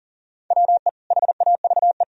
Morse code, heard as a keyed tone, about 40 wpm, 700 Hz. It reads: WE HAVE
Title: Sentences from Top 1000 Words - Review Part 1 - 40wpm